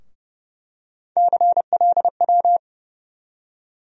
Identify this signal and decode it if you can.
Morse code, heard as a keyed tone, about 30 wpm, 700 Hz. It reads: CLW
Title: USA Call Sign Postfixes - 3 Letters 30wpm